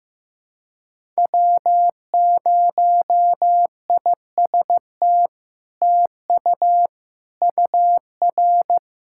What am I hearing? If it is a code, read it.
Morse code, heard as a keyed tone, about 16 wpm, 700 Hz. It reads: W0IST TU UR
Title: SOTA QSO Practice 15wpm